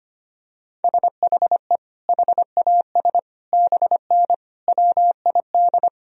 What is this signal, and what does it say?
Morse code, heard as a keyed tone, about 25 wpm, 700 Hz. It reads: SHE HAS BN WID
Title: Sentences from Top 200 Words and Abbreviations 25wpm